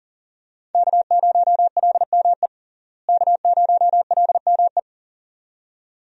Morse code, heard as a keyed tone, about 37 wpm, 700 Hz. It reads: K0LME K0LME
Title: USA Call Signs - 5 Characters - Repeated 2x - 40wpm